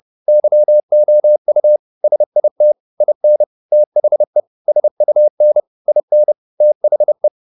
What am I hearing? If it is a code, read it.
Morse code, heard as a keyed tone, about 30 wpm, 600 Hz. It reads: YOU SIT IN THE SUN IN THE
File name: the old man and the sea / the old man and the sea 1 30wpm